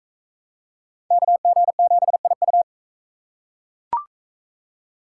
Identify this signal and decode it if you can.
Morse code, heard as a keyed tone, about 40 wpm, 700 Hz. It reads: KG7IU